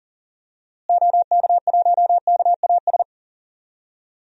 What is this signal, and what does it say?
Morse code, heard as a keyed tone, about 40 wpm, 700 Hz. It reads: OK1KAS
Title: DXCC-100 Call Signs - 6 Characters - Repeated 3x - 40wpm